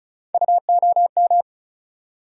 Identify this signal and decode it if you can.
Morse code, heard as a keyed tone, about 36 wpm, 700 Hz. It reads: UOM
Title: USA Call Sign Postfixes - 3 Letters 35wpm